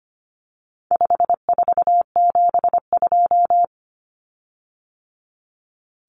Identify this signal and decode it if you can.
Morse code, heard as a keyed tone, about 25 wpm, 700 Hz. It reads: 5472
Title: Common Cut Numbers - 4-Digit 25wpm